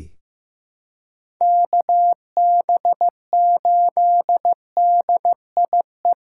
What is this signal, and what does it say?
Morse code, heard as a keyed tone, about 15 wpm, 700 Hz. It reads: KB8DIE
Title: USA Call Signs - 6 Characters - 2x Character Spacing - 15wpm